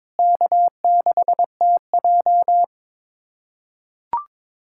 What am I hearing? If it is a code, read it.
Morse code, heard as a keyed tone, about 22 wpm, 700 Hz. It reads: K6TJ
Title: K1USN SST QSO Practice - 22wpm